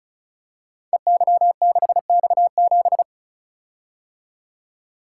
Morse code, heard as a keyed tone, about 34 wpm, 700 Hz. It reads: EY6X7
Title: Code Groups - 5 Letter:Numbers - 35wpm